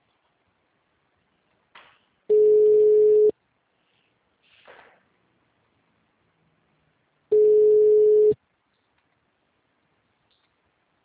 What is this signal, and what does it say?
Ringback tone (Europe, 425 Hz).